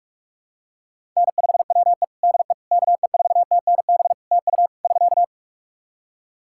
Morse code, heard as a keyed tone, about 46 wpm, 700 Hz. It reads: N5WE DE KE4TNB TU <SK>
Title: K1USN SST QSO Practice 45wpm